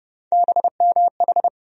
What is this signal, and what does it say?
Morse code, heard as a keyed tone, about 30 wpm, 700 Hz. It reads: BMH
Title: USA Call Sign Postfixes - 3 Letters 30wpm